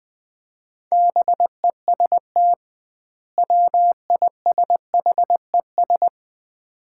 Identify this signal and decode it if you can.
Morse code, heard as a keyed tone, about 20 wpm, 700 Hz. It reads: BEST WISHES
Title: Common QSO Phrases - 2x Word-Spacing - 20wpm